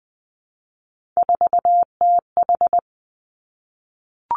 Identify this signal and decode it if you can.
Morse code, heard as a keyed tone, about 19 wpm, 700 Hz. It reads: 4TH
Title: Three Challenging Letter-Number Combinations 20wpm